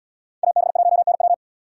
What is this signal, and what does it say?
Morse code, heard as a keyed tone, about 77 wpm, 700 Hz. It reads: AB1NW